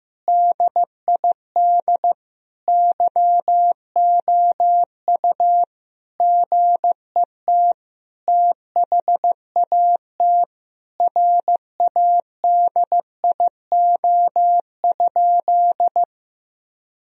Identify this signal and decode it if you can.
Morse code, heard as a keyed tone, about 15 wpm, 700 Hz. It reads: DID YOU GET THAT RADIO?
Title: Sentences from Top 1000 Words - Review Part 3 - 15wpm